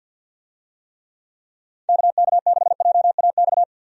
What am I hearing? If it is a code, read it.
Morse code, heard as a keyed tone, about 51 wpm, 700 Hz. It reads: KK6JAX
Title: USA Call Signs - 6 Characters - 50wpm